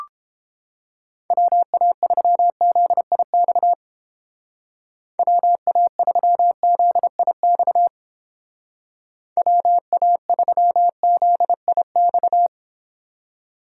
Morse code, heard as a keyed tone, about 29 wpm, 700 Hz. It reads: WA3ZIX WA3ZIX WA3ZIX